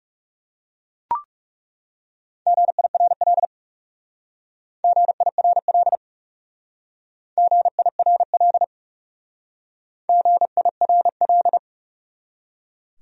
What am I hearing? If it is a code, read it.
Morse code, heard as a keyed tone, about 36 wpm, 700 Hz. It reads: GIRL GIRL GIRL GIRL
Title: Speed-Racing Top 500 Words 30wpm